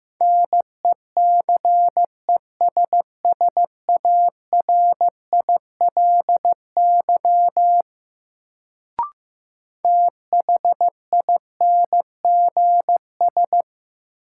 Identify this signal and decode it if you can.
Morse code, heard as a keyed tone, about 15 wpm, 700 Hz. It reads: NECESSARILY THINGS
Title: Sentences from Top 1400 Words - New Word Drill Alt - 15wpm